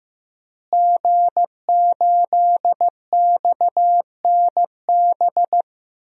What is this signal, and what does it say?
Morse code, heard as a keyed tone, about 15 wpm, 700 Hz. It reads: G8XNB
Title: DXCC-100 Call Signs - 5 Characters - Repeated 2x - 15wpm